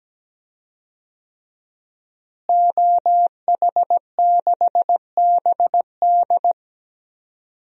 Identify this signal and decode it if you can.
Morse code, heard as a keyed tone, about 17 wpm, 700 Hz. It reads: OH6BD
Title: DXCC-100 Call Signs - 5 Characters - Repeated 3x - 17wpm